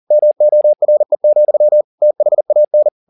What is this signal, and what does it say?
Morse code, heard as a keyed tone, about 40 wpm, 600 Hz. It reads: MORE, THAN
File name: the old man and the sea 2 40wpm